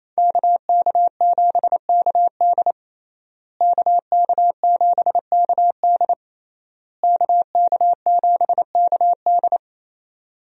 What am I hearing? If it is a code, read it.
Morse code, heard as a keyed tone, about 28 wpm, 700 Hz. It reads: KK7KD KK7KD KK7KD
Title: USA Call Signs - 5 Characters - Repeated 3x - 28wpm